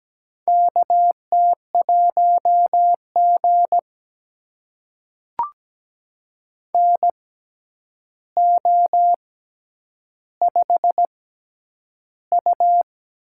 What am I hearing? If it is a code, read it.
Morse code, heard as a keyed tone, about 17 wpm, spaced as for 9 wpm, 700 Hz. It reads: KT1G NO5U
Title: USA Call Signs - 4 Characters - 6x Character Spacing - 17wpm